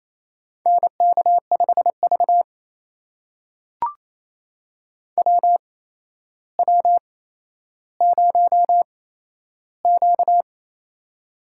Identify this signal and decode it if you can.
Morse code, heard as a keyed tone, about 28 wpm, spaced as for 12 wpm, 700 Hz. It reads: NK5V WW0Q
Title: USA Call Signs - 8x Character Spacing - 28wpm